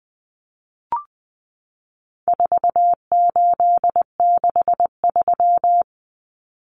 Morse code, heard as a keyed tone, about 19 wpm, 700 Hz. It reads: 4863
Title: Common Cut Numbers - 4-Digit 20wpm